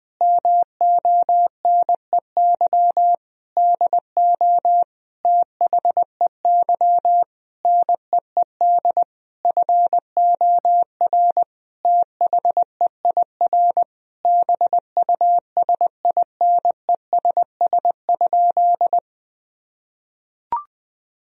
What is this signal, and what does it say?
Morse code, heard as a keyed tone, about 20 wpm, 700 Hz. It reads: MONEY DO THEY NEED FOR THEIR BUSINESS?
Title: Sentences from Top 1000 Words - Review Part 4 - 20wpm